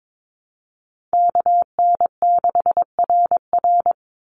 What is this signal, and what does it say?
Morse code, heard as a keyed tone, about 22 wpm, 700 Hz. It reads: KN6RR
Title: USA Call Signs 22wpm